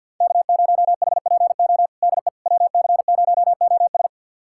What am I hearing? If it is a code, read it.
Morse code, heard as a keyed tone, about 48 wpm, 700 Hz. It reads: K0HPO DE WC0OS
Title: USA Callsign DE Callsign - 5 Characters - 50wpm